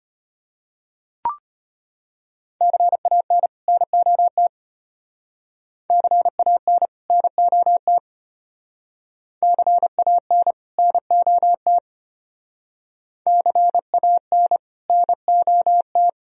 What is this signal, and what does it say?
Morse code, heard as a keyed tone, about 32 wpm, 700 Hz. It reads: CAN NOT CAN NOT CAN NOT CAN NOT